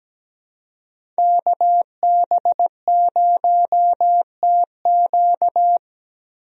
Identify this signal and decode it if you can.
Morse code, heard as a keyed tone, about 17 wpm, 700 Hz. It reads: KB0TQ
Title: USA Call Signs - 5 Characters - Repeated 3x - 17wpm